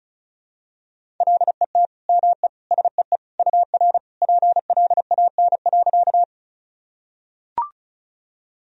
Morse code, heard as a keyed tone, about 36 wpm, 700 Hz. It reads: LET ME SEE UR PLAN.
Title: Sentences from Top 500 Words and Abbreviations 35wpm